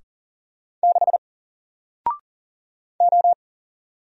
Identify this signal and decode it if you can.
Morse code, heard as a keyed tone, about 40 wpm, 700 Hz. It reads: 6 O
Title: Single Letter-Number Alt 40wpm